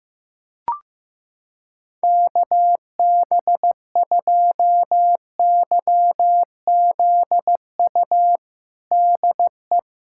Morse code, heard as a keyed tone, about 15 wpm, 700 Hz. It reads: KB2YZU DE
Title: USA Callsign DE Callsign 15wpm